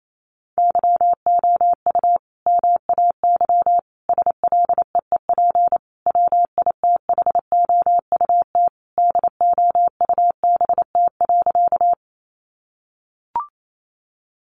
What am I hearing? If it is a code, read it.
Morse code, heard as a keyed tone, about 28 wpm, 700 Hz. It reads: YOU MAY SLEEP WITHOUT DOUBT.
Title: Sentences from Top 1500 Words 28wpm